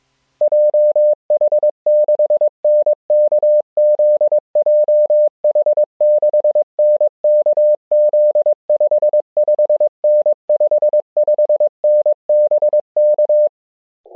Morse code, heard as a keyed tone, about 22 wpm, 600 Hz. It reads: JH6NKZ JH6NKZ 55N 55N BK
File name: jh6nkz-g8bcg